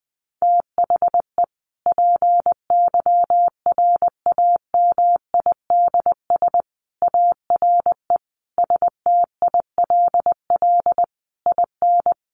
Morse code, heard as a keyed tone, about 20 wpm, 700 Hz. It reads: THE PYRAMIDS ARE STILL IN
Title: A Sound Of Thunder 20wpm